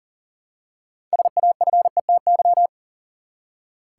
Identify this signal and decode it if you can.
Morse code, heard as a keyed tone, about 38 wpm, 700 Hz. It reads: SAFETY
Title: Six Letter Words - 40wpm